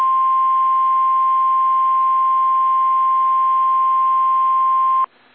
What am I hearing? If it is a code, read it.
Emergency Alert System (SAME) tones (the 1050 Hz attention tone). It is heard from 0.3 to 4.7 s.
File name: eas / KSDZKDJL / KSDZKDJL